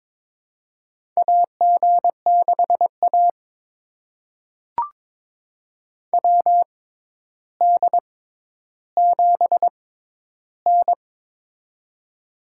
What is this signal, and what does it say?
Morse code, heard as a keyed tone, about 21 wpm, spaced as for 11 wpm, 700 Hz. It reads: AG6A WD7N
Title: USA Call Signs - 6x Character Spacing - 22wpm